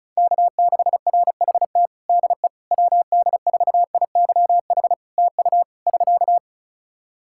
Morse code, heard as a keyed tone, about 36 wpm, 700 Hz. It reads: K6RHT DE WD4IYH TU <SK>
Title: Practice QSO - Exchange 4 - Ending - 35wpm